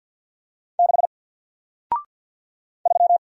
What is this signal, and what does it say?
Morse code, heard as a keyed tone, about 48 wpm, 700 Hz. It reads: 6 3
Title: Single Letter-Number Alt 50wpm